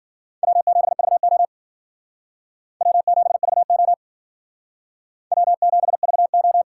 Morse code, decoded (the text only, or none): W7VO W7VO W7VO